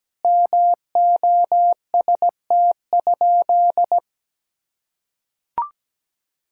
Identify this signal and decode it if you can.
Morse code, heard as a keyed tone, about 17 wpm, 700 Hz. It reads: MOST?
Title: Sentences from Top 1000 Words - Review Part 5 - 17wpm